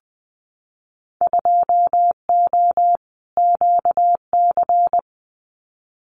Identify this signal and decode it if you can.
Morse code, heard as a keyed tone, about 20 wpm, 700 Hz. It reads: 2O QC
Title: Winter Field Day Exchanges 20wpm